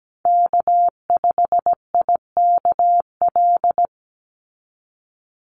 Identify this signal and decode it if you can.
Morse code, heard as a keyed tone, about 17 wpm, 700 Hz. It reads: K5IKL
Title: USA Call Signs 17wpm